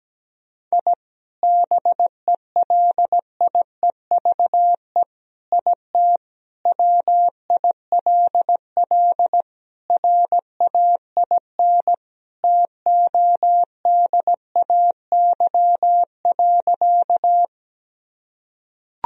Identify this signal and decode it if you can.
Morse code, heard as a keyed tone, about 17 wpm, 700 Hz. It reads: I BELIEVE IT WILL RAIN TODAY.
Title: Sentences from Top 1000 Words - Review Part 1 - 17wpm